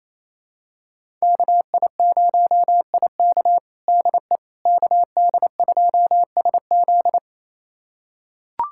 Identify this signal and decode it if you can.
Morse code, heard as a keyed tone, about 28 wpm, 700 Hz. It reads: KI0IK DE KD2SZ
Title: USA Callsign DE Callsign - 5 Characters - 28wpm